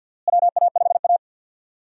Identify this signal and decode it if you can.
Morse code, heard as a keyed tone, about 47 wpm, 700 Hz. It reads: WA5A
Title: USA Call Signs - 4 Characters - Repeated 2x - 50wpm